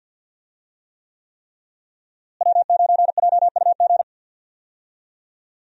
Morse code, heard as a keyed tone, about 51 wpm, 700 Hz. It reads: W9JUG
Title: USA Call Signs - 5 Characters - Repeated 3x - 50wpm